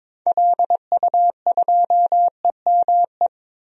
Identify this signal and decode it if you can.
Morse code, heard as a keyed tone, about 22 wpm, 700 Hz. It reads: LU2EME
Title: DXCC-100 Call Signs - 6 Characters - 22wpm